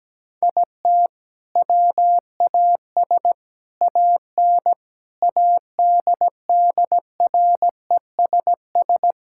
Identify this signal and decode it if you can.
Morse code, heard as a keyed tone, about 17 wpm, 700 Hz. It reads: IT WAS AN ADDRESS
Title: Sentences from Top 1000 Words - Review Part 6 - 17wpm